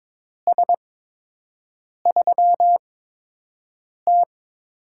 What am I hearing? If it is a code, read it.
Morse code, heard as a keyed tone, about 22 wpm, spaced as for 6 wpm, 700 Hz. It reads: S3T
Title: USA Call Signs - 4 Characters - 8x Character Spacing - 22wpm